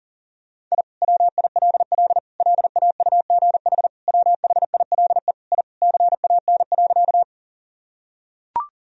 Morse code, heard as a keyed tone, about 40 wpm, 700 Hz. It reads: I WILL LAUGH WHILE I CAN.
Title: Sentences from Top 1000 Words - Review Part 1 - 40wpm